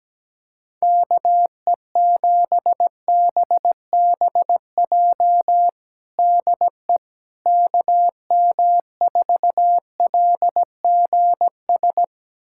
Morse code, heard as a keyed tone, about 17 wpm, 700 Hz. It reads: KE7BBJ DE KM4LGS
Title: USA Callsign DE Callsign 17wpm